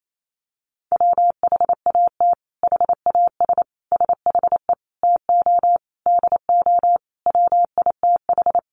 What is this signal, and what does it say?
Morse code, heard as a keyed tone, about 28 wpm, 700 Hz. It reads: WHAT HAS SHE TO DO WITH
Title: Sentences from Top 100 Words 28wpm